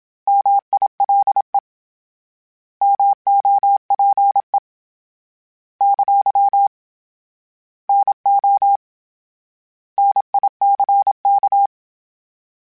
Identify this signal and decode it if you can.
Morse code, heard as a keyed tone, about 27 wpm, 800 Hz. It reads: MILE MOPE ! NO NICK
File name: Misc / Misc-mp3s / Misc @25